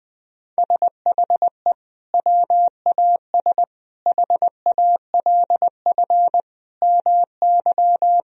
Morse code, heard as a keyed tone, about 20 wpm, 700 Hz. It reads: SHE WAS HALF MY
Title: Sentences from Top 1000 Words - Review Part 2 - 20wpm